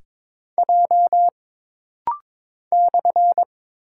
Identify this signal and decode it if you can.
Morse code, heard as a keyed tone, about 22 wpm, 700 Hz. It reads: J /
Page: This rapid-fire timing is geared to push your ICR (Instant Character Recognition) skills.